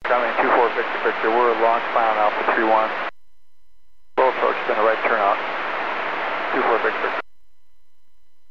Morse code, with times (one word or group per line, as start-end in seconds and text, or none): none